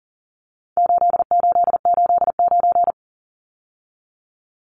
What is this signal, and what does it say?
Morse code, heard as a keyed tone, about 40 wpm, 700 Hz. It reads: 8889
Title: Numbers - 4-Digit 40wpm